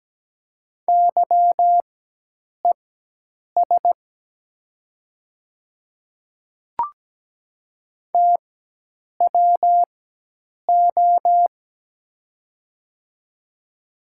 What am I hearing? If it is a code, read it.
Morse code, heard as a keyed tone, about 17 wpm, spaced as for 8 wpm, 700 Hz. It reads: YES TWO
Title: 4x Character Spacing - Top 500 Words - 17wpm